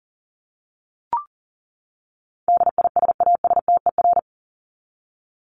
Morse code, heard as a keyed tone, about 40 wpm, 700 Hz. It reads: DISASTER
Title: Top 1501-1750 Words 40wpm